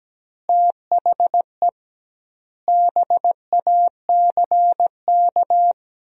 Morse code, heard as a keyed tone, about 17 wpm, 700 Hz. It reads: THE BACK
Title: 2x Word-Spacing - Sentences from Top 300 Words - 17wpm